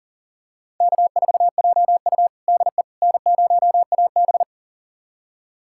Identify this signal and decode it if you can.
Morse code, heard as a keyed tone, about 40 wpm, 700 Hz. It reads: K4JU DE N0AB
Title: USA Callsign DE Callsign - 4 Characters - 40wpm